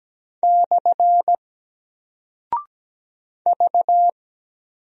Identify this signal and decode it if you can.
Morse code, heard as a keyed tone, about 17 wpm, 700 Hz. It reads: / V
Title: Single Character (Letter:Number:Punctuation:Prosign) Alt 17wpm